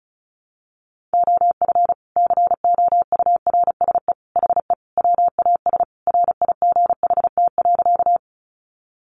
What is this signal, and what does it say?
Morse code, heard as a keyed tone, about 34 wpm, 700 Hz. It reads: OF COURSE HE WAS RIGHT.
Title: Sentences from Top 500 Words 35wpm